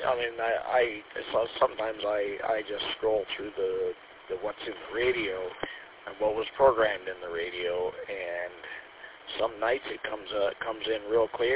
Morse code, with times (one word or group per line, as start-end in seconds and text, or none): none